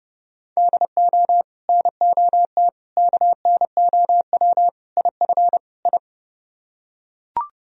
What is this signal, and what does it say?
Morse code, heard as a keyed tone, about 32 wpm, 700 Hz. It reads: DO NOT KNOW IF I